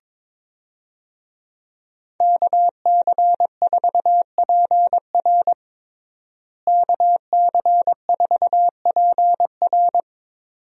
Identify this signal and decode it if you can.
Morse code, heard as a keyed tone, about 22 wpm, 700 Hz. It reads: KC4PR KC4PR
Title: USA Call Signs - 5 Characters - Repeated 2x - 22wpm